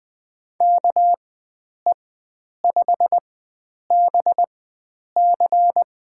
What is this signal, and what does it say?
Morse code, heard as a keyed tone, about 20 wpm, spaced as for 9 wpm, 700 Hz. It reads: KE5BC
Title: USA Call Signs - 4x Character Spacing - 20wpm